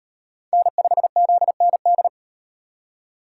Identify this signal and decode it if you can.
Morse code, heard as a keyed tone, about 38 wpm, 700 Hz. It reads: N5ZND